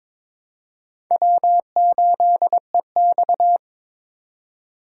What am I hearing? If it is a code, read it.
Morse code, heard as a keyed tone, about 21 wpm, 700 Hz. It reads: W8EX
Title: USA Call Signs - 4 Characters - 2x Character Spacing - 22wpm